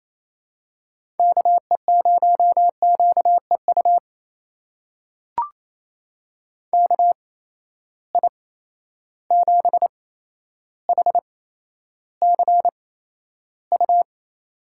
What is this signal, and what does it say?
Morse code, heard as a keyed tone, about 28 wpm, spaced as for 12 wpm, 700 Hz. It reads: KE0QEU KI7HCU
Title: USA Call Signs - 6 Characters - 8x Character Spacing - 28wpm